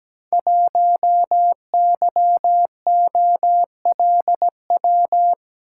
Morse code, heard as a keyed tone, about 17 wpm, 700 Hz. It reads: 1YOLW
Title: Code Groups - 5 Letter:Numbers - 17wpm